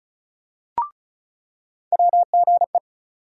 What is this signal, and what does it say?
Morse code, heard as a keyed tone, about 34 wpm, 700 Hz. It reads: WGE
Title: USA Call Sign Postfixes - 3 Letters 35wpm